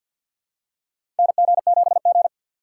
Morse code, heard as a keyed tone, about 48 wpm, 700 Hz. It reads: NG7G
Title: USA Call Signs - 4x Character Spacing - 50wpm